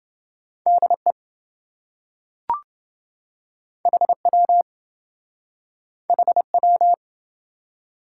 Morse code, heard as a keyed tone, about 27 wpm, 700 Hz. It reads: DE HW HW